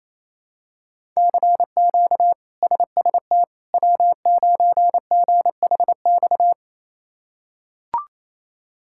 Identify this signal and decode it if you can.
Morse code, heard as a keyed tone, about 28 wpm, 700 Hz. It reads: CQ SST W9GHX
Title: K1USN SST QSO Practice - 28wpm